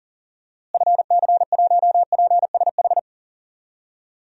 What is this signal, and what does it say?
Morse code, heard as a keyed tone, about 40 wpm, 700 Hz. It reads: FC1PSH
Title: DXCC-100 Call Signs - 6 Characters - Repeated 2x - 40wpm